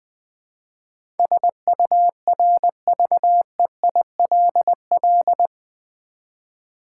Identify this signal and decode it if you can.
Morse code, heard as a keyed tone, about 20 wpm, 700 Hz. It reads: SURVEILL
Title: Build A Word - 4 Letters At A Time - 12-Letter Words - 20wpm